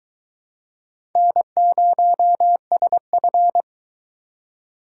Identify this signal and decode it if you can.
Morse code, heard as a keyed tone, about 23 wpm, 700 Hz. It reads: N0SF